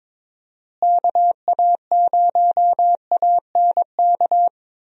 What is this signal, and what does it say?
Morse code, heard as a keyed tone, about 22 wpm, 700 Hz. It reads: KA0ANK
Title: USA Call Signs - 6 Characters - Repeated 3x - 22wpm